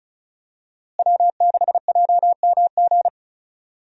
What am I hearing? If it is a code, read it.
Morse code, heard as a keyed tone, about 34 wpm, 700 Hz. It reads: W6JMG
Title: USA Call Signs - 5 Characters - Repeated 2x - 35wpm